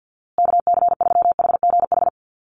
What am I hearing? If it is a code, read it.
Morse code, heard as a keyed tone, about 50 wpm, 700 Hz. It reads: KC3HGH
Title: USA Call Signs Repeated 2x 50wpm